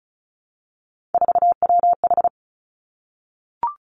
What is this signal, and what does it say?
Morse code, heard as a keyed tone, about 36 wpm, 700 Hz. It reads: 4WH
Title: Three Challenging Letter-Number Combinations 35wpm